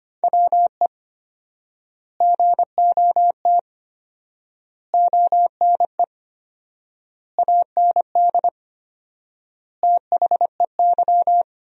Morse code, heard as a keyed tone, about 25 wpm, 700 Hz. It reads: WE GOT ONE AND THEY
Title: 4x Word-Spacing - Sentences from Top 300 Words - 25wpm